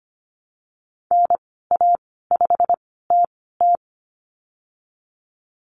Morse code, heard as a keyed tone, about 25 wpm, spaced as for 16 wpm, 700 Hz. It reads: NA5TT
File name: Lesson 15g - Number 5 - Call Signs 25wpm